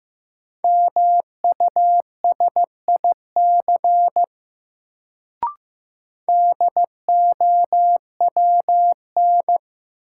Morse code, heard as a keyed tone, about 16 wpm, 700 Hz. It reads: MUSIC DOWN
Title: Sentences from Top 300 Words - New Word Drill - 15wpm